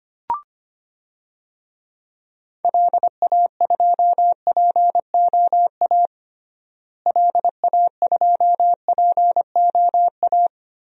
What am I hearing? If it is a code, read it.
Morse code, heard as a keyed tone, about 25 wpm, 700 Hz. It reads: LA2POA LA2POA
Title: DXCC-100 Call Signs - 6 Characters - Repeated 2x - 25wpm